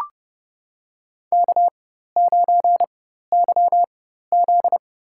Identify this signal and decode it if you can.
Morse code, heard as a keyed tone, about 30 wpm, spaced as for 14 wpm, 700 Hz. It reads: K9YZ
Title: USA Call Signs - 4 Characters - 4x Character Spacing - 30wpm